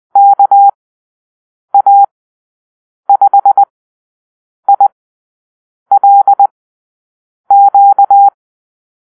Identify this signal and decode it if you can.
Morse code, heard as a keyed tone, about 20 wpm, spaced as for 7 wpm, 800 Hz. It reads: KA5ILQ
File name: arrl / Call-Sign Drill-9